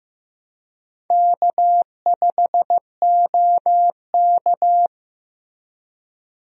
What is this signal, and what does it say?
Morse code, heard as a keyed tone, about 15 wpm, 700 Hz. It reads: K5OK
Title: USA Call Signs - 4 Characters - Repeated 2x - 15wpm